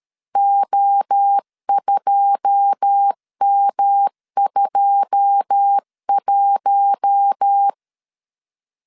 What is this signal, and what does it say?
Morse code, heard as a keyed tone, about 13 wpm, 800 Hz. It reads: O2M21
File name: CW / DET / 1 9 1500C S C 12wpm 3811